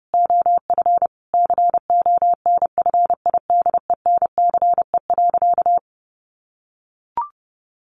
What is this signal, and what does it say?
Morse code, heard as a keyed tone, about 30 wpm, 700 Hz. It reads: OF CONFIDENCE.
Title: Sentences from Top 1500 Words 30wpm